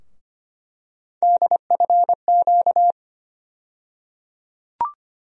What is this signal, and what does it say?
Morse code, heard as a keyed tone, about 25 wpm, 700 Hz. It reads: DFQ
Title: USA Call Sign Postfixes - 3 Letters 25wpm